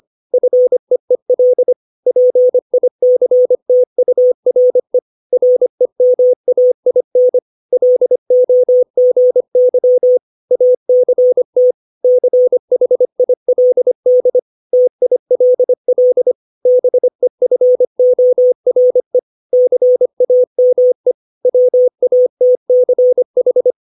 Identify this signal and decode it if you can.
Morse code, heard as a keyed tone, about 25 wpm, 500 Hz. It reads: FEEL PICTURE REMAIN LOGY ACT CHILD TILL BEFORE CAME WATCH
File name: 632 words / 632 Words 25WPM